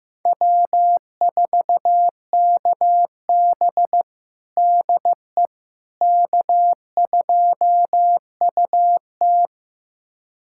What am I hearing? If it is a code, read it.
Morse code, heard as a keyed tone, about 15 wpm, 700 Hz. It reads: W4KB DE K2UT
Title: USA Callsign DE Callsign - 4 Characters - 15wpm